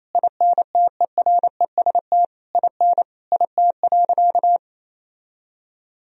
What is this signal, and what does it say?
Morse code, heard as a keyed tone, about 28 wpm, 700 Hz. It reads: INTEREST IN IT.
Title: Sentences from Top 1000 Words - Review Part 4 - 28wpm